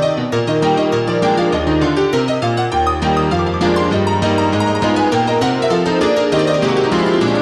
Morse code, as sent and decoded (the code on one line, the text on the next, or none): none